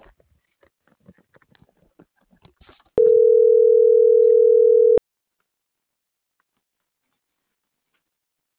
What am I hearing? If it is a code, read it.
Ringback tone (North America), 440 + 480 Hz.